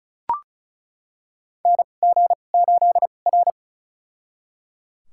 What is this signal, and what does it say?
Morse code, heard as a keyed tone, about 36 wpm, spaced as for 25 wpm, 700 Hz. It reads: NG8R
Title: USA Call Signs - 4 Characters - 2x Character Spacing - 35wpm